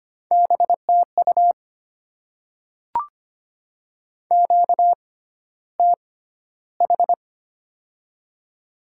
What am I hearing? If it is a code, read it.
Morse code, heard as a keyed tone, about 25 wpm, spaced as for 13 wpm, 700 Hz. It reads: BTU QTH
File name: Top 20 QSO Elements - 6x Character Spacing - 25wpm